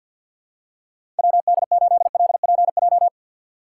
Morse code, heard as a keyed tone, about 52 wpm, 700 Hz. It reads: WD8LPJ
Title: USA Call Signs - 6x Character Spacing - 50wpm